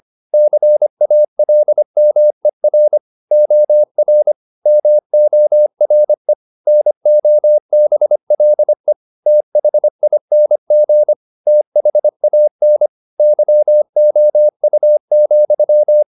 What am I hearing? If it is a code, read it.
Morse code, heard as a keyed tone, about 25 wpm, 600 Hz. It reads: CALMER OR MORE NOBLE THING THAN YOU,